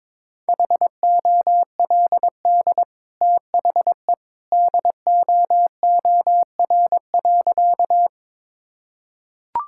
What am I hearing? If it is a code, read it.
Morse code, heard as a keyed tone, about 21 wpm, 690 Hz. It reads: HOLD THE DOOR.
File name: practice / 22wpm / Easy Sentences with Hints - 22wpm